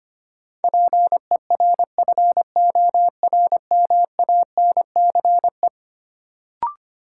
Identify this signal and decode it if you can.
Morse code, heard as a keyed tone, about 25 wpm, 700 Hz. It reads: PERFORMANCE
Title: Sentences from Top 1000 Words - New Word Drill - 25wpm